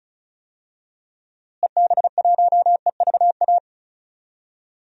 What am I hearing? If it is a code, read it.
Morse code, heard as a keyed tone, about 36 wpm, 700 Hz. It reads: EB1EVA
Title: DXCC-100 Call Signs - 6 Characters - Repeated 2x - 35wpm